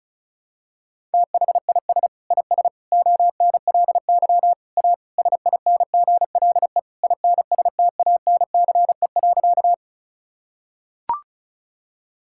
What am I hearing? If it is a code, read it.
Morse code, heard as a keyed tone, about 34 wpm, 700 Hz. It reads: THIS IS ONLY A SINGLE INSTANCE.
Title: Sentences from Top 1000 Words - Review Part 5 - 35wpm